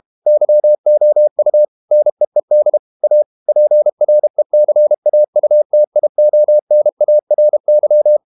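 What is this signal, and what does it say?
Morse code, heard as a keyed tone, about 32 wpm, 600 Hz. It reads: YOU NEED A PRECAUTIONARY
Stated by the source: Morse code practice at 32 wpm from CW med Gnister, reading Ken Follett's novel Never in English.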